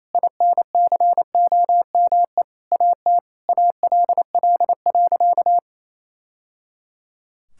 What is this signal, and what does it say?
Morse code, heard as a keyed tone, about 28 wpm, 700 Hz. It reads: INCOME AT ALL.
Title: Sentences from Top 1000 Words - Review Part 5 - 28wpm